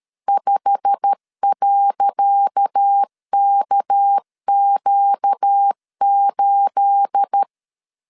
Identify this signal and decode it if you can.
Morse code, heard as a keyed tone, about 13 wpm, 800 Hz. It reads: 5.KQ8
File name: LEZIONI / 10 SIGNS / 10C DOT 12wpm 2451